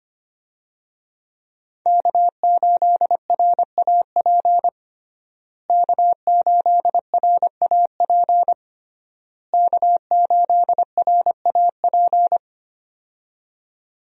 Morse code, heard as a keyed tone, about 25 wpm, 700 Hz. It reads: K8RAP K8RAP K8RAP
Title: USA Call Signs - 5 Characters - Repeated 3x - 25wpm